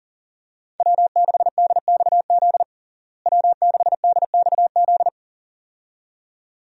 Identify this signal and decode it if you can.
Morse code, heard as a keyed tone, about 40 wpm, 700 Hz. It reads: W6DXZ W6DXZ
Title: USA Call Signs - 5 Characters - Repeated 2x - 40wpm